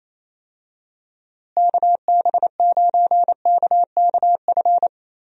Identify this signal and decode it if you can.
Morse code, heard as a keyed tone, about 28 wpm, 700 Hz. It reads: KB9KKF
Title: USA Call Signs - 6 Characters - Repeated 2x - 28wpm